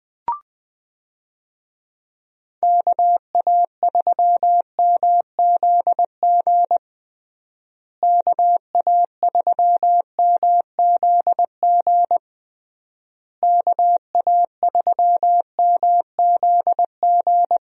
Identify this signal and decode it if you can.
Morse code, heard as a keyed tone, about 21 wpm, 700 Hz. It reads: KA3MZG KA3MZG KA3MZG
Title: USA Call Signs - 6 Characters - Repeated 3x - 20wpm